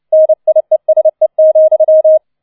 Morse code, heard as a keyed tone, about 29 wpm, 630 Hz. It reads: NIESE,